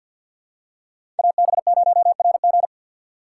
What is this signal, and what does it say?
Morse code, heard as a keyed tone, about 52 wpm, 700 Hz. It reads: AB0RG
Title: USA Call Signs - 3x Character Spacing - 50wpm